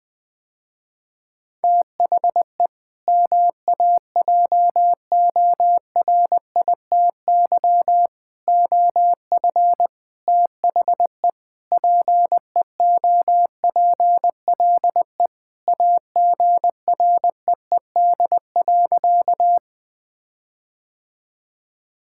Morse code, heard as a keyed tone, about 21 wpm, 700 Hz. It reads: THE MAJORITY OF THE PEOPLE AGREED.
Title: Sentences with 4-Word Phrases - Spoken Beginning - 20wpm